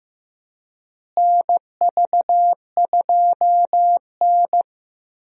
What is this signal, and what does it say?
Morse code, heard as a keyed tone, about 15 wpm, 700 Hz. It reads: NV2N
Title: USA Call Signs - 4 Characters - Repeated 3x - 15wpm